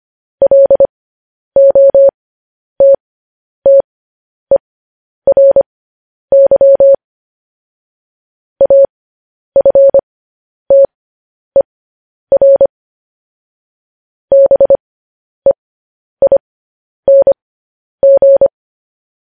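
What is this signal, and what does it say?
Morse code, heard as a keyed tone, about 25 wpm, spaced as for 10 wpm, 550 Hz. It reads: LOTTERY AFTER BEING
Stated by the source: Morse code news headlines.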